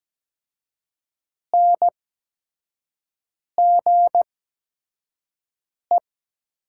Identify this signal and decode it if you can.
Morse code, heard as a keyed tone, about 17 wpm, spaced as for 5 wpm, 700 Hz. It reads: NGE
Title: 8x Character Spacing - Top 200 Words - 17wpm